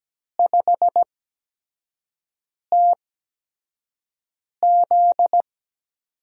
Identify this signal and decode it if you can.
Morse code, heard as a keyed tone, about 17 wpm, spaced as for 5 wpm, 700 Hz. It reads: 5TZ
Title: USA Call Signs - 5 Characters - 8x Character Spacing - 17wpm